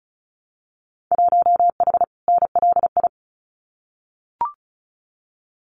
Morse code, heard as a keyed tone, about 36 wpm, 700 Hz. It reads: 1H NLI
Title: Winter Field Day Exchanges 35wpm